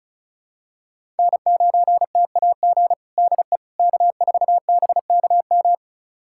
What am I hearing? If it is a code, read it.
Morse code, heard as a keyed tone, about 34 wpm, 690 Hz. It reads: N9TAG DE K4BKM
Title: USA Callsign DE Callsign - 5 Characters - 35wpm